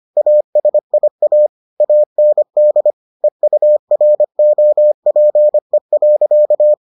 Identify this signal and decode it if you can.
Morse code, heard as a keyed tone, about 25 wpm, 600 Hz. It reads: ASIA AND EUROPE.